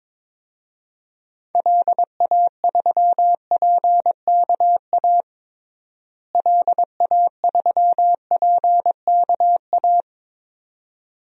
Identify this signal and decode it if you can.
Morse code, heard as a keyed tone, about 22 wpm, 700 Hz. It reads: LA3PKA LA3PKA
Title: DXCC-100 Call Signs - 6 Characters - Repeated 2x - 22wpm